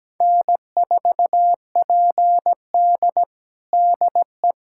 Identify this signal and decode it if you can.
Morse code, heard as a keyed tone, about 17 wpm, 700 Hz. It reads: N4PD DE
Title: USA Callsign DE Callsign - 4 Characters - 17wpm